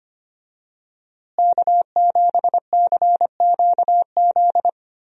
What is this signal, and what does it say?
Morse code, heard as a keyed tone, about 25 wpm, 700 Hz. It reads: K7CQZ
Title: USA Call Signs - 5 Characters - Repeated 3x - 25wpm